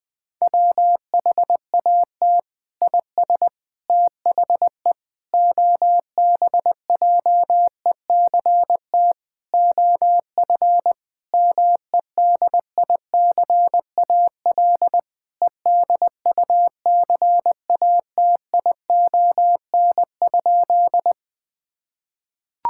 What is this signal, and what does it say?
Morse code, heard as a keyed tone, about 20 wpm, 700 Hz. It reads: WHAT IS THE OBJECT OF MEDICAL EDUCATION?
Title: Sentences from Top 1000 Words - Review Part 6 - 20wpm